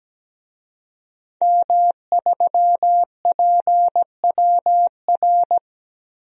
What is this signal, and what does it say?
Morse code, heard as a keyed tone, about 17 wpm, 700 Hz. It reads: M3PWR
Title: DXCC-100 Call Signs - 5 Characters - Repeated 2x - 17wpm